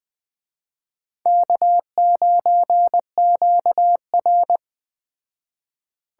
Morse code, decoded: K9QR

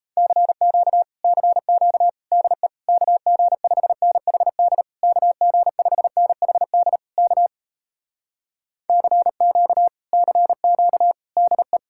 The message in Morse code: CQ CQ DE KG5NHD KG5NHD K CQ CQ DE

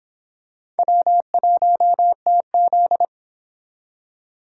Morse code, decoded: W1TZ

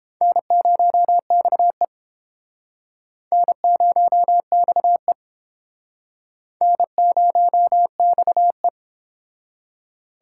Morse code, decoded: N0XE N0XE N0XE

